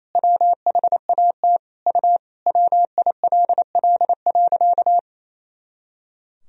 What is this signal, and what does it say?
Morse code, heard as a keyed tone, about 27 wpm, 700 Hz. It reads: WHAT U WILL.
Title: Sentences from Top 200 Words and Abbreviations 28wpm